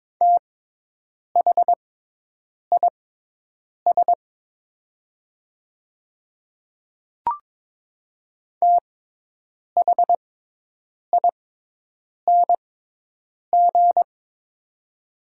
Morse code, heard as a keyed tone, about 22 wpm, spaced as for 8 wpm, 700 Hz. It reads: THIS THING
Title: 6x Character Spacing - Top 100 Words - 22wpm